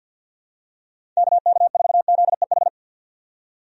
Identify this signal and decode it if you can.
Morse code, heard as a keyed tone, about 51 wpm, 700 Hz. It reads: KK4ZEH
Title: USA Call Signs - 6 Characters - 50wpm